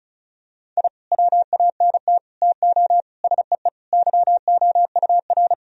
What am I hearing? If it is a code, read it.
Morse code, heard as a keyed tone, about 36 wpm, 700 Hz. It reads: I WANT TO SEE YOUR
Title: Sentences from Top 1000 Words - Review Part 1 - 35wpm